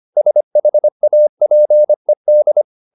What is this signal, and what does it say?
Morse code, heard as a keyed tone, about 25 wpm, 600 Hz. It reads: SHAPED